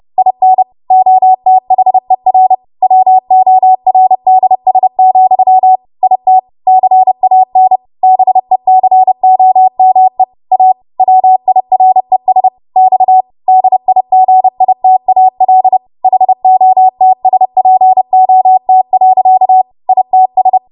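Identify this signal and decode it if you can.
Morse code, heard as a keyed tone, about 30 wpm, 750 Hz. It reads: IN OTHER WORDS, IT CAN BECOME A WIRES X DIGITAL HOTSPOT. ITS
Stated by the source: Morse practice transmission at 30 WPM.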